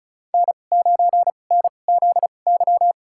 Morse code, heard as a keyed tone, about 36 wpm, spaced as for 25 wpm, 700 Hz. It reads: N9NZY
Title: USA Call Signs - 5 Characters - 2x Character Spacing - 35wpm